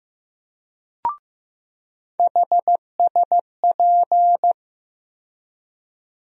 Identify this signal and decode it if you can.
Morse code, heard as a keyed tone, about 15 wpm, 700 Hz. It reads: HSP
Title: USA Call Sign Postfixes - 3 Letters 15wpm